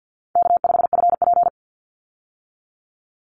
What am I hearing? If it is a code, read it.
Morse code, heard as a keyed tone, about 50 wpm, 700 Hz. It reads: K5FP
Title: Easy Call Signs Repeated 3x - 50wpm